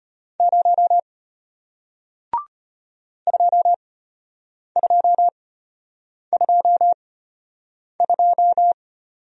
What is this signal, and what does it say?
Morse code, heard as a keyed tone, about 34 wpm, spaced as for 8 wpm, 700 Hz. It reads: 0 2222